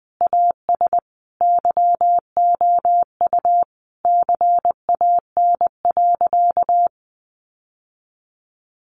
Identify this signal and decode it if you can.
Morse code, heard as a keyed tone, about 20 wpm, 700 Hz. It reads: AS YOU CAN.
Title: Sentences from Top 1200 Words 20wpm